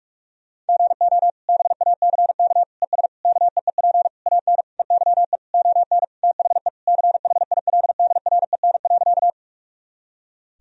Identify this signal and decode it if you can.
Morse code, heard as a keyed tone, about 46 wpm, 700 Hz. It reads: GO BACK ES KEEP AN EYE ON THE CHILDREN.
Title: Sentences from Top 300 Words and Abbreviations 45wpm